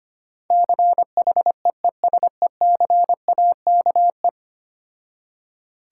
Morse code, heard as a keyed tone, about 25 wpm, 700 Hz. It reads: CHEESECAKE
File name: practice / 25wpm / Difficult Words with Hints - 25wpm